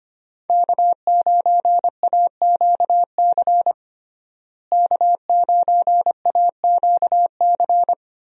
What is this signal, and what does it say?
Morse code, heard as a keyed tone, about 25 wpm, 700 Hz. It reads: K9AQC K9AQC
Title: USA Call Signs - 5 Characters - Repeated 2x - 25wpm